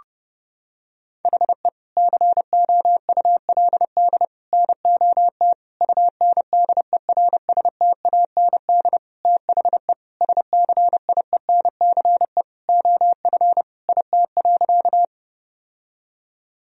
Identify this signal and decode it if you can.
Morse code, heard as a keyed tone, about 30 wpm, 700 Hz. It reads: HE COULD NOT UNDERSTAND THE SCIENCE OF IT.
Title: Sentences from Top 1000 Words - Review Part 2 - 30wpm